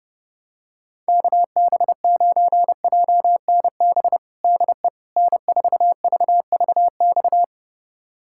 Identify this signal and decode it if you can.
Morse code, heard as a keyed tone, about 30 wpm, 700 Hz. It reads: KB9JNB DE N4VVX
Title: USA Callsign DE Callsign 30wpm